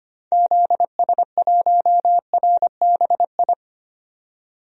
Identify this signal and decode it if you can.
Morse code, heard as a keyed tone, about 25 wpm, 700 Hz. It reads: ZS1RBI
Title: DXCC-100 Call Signs - 6 Characters - 25wpm